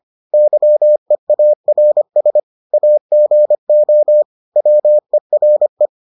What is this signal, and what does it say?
Morse code, heard as a keyed tone, about 25 wpm, 600 Hz. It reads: YEARS AGO WERE